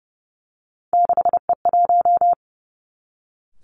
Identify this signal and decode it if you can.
Morse code, heard as a keyed tone, about 30 wpm, 700 Hz. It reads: 6E1
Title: Three Challenging Letter-Number Combinations 30wpm